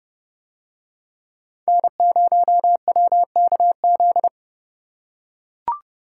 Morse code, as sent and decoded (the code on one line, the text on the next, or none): -. ----- .-- -.- --..
N0WKZ